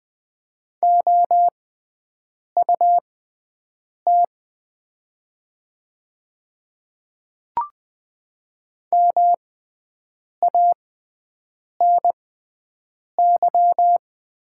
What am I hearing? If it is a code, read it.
Morse code, heard as a keyed tone, about 20 wpm, spaced as for 7 wpm, 700 Hz. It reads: OUT MANY
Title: 6x Character Spacing - Top 100 Words - 20wpm